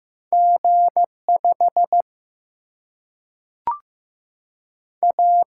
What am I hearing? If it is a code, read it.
Morse code, heard as a keyed tone, about 15 wpm, 700 Hz. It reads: G5 A
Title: Single-Letter Single-Number 15wpm